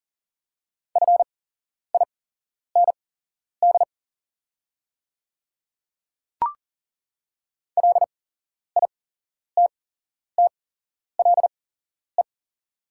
Morse code, heard as a keyed tone, about 40 wpm, spaced as for 11 wpm, 700 Hz. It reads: FIND LITTLE
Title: 8x Character Spacing - Top 300 Words - 40wpm